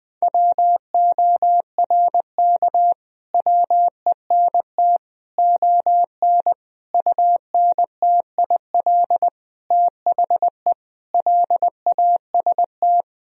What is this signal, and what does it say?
Morse code, heard as a keyed tone, about 20 wpm, 700 Hz. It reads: WORK WENT ON UNTIL THE LAST
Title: Sentences from Top 1000 Words - Review Part 1 - 20wpm